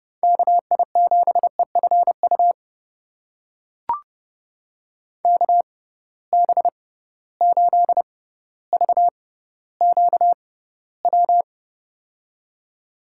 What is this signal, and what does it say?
Morse code, heard as a keyed tone, about 30 wpm, spaced as for 15 wpm, 700 Hz. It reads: KI7EFU KB8VQW
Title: USA Call Signs - 6 Characters - 6x Character Spacing - 30wpm